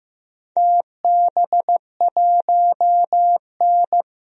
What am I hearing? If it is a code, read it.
Morse code, heard as a keyed tone, about 15 wpm, 700 Hz. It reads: TB1N
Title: DXCC-100 Call Signs - 4 Characters - 15wpm